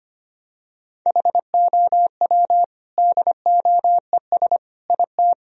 Morse code, heard as a keyed tone, about 25 wpm, 700 Hz. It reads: HOW DOES IT
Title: Sentences from Top 1000 Words - Review Part 2 - 25wpm